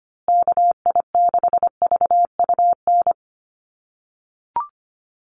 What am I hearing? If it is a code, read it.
Morse code, heard as a keyed tone, about 25 wpm, 700 Hz. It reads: KI6VUN
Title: USA Call Signs Repeated 2x 25wpm